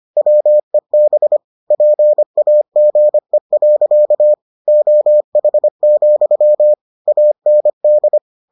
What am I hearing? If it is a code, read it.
Morse code, heard as a keyed tone, about 25 wpm, 600 Hz. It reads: WEB PAGE. OH, AND